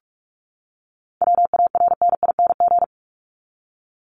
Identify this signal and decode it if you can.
Morse code, heard as a keyed tone, about 44 wpm, 700 Hz. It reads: WARNING
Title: Top 1751-2000 Words 45wpm